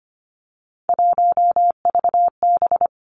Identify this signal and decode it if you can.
Morse code, heard as a keyed tone, about 25 wpm, 700 Hz. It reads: 1VB
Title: Three Challenging Letter-Number Combinations 25wpm